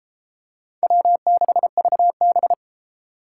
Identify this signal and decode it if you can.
Morse code, heard as a keyed tone, about 32 wpm, 700 Hz. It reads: W6VB